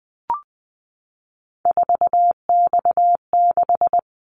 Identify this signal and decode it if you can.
Morse code, heard as a keyed tone, about 20 wpm, 700 Hz. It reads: 4X6
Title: Callsign Prefixes for Top 50 DXCC - 20wpm